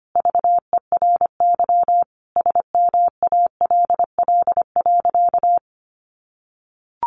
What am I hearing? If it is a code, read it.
Morse code, heard as a keyed tone, about 25 wpm, 700 Hz. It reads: VERY SMALL.
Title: Sentences from Top 400 Words 25wpm